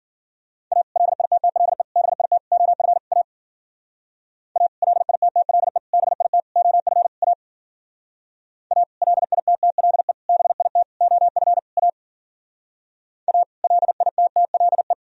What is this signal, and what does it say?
Morse code, heard as a keyed tone, about 52 wpm, 700 Hz. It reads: A LITTLE BIT OF A A LITTLE BIT OF A A LITTLE BIT OF A A LITTLE